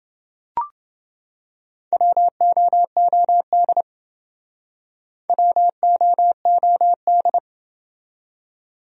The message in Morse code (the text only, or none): WOOD WOOD